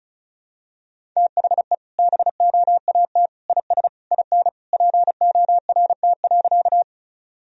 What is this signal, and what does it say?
Morse code, heard as a keyed tone, about 34 wpm, 700 Hz. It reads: THE BOAT IS IN PORT.
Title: Sentences from Top 1000 Words - Review Part 1 - 35wpm